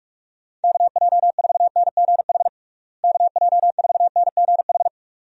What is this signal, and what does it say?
Morse code, heard as a keyed tone, about 46 wpm, 700 Hz. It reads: KJ4NGH KJ4NGH
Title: USA Call Signs - 6 Characters - Repeated 2x - 45wpm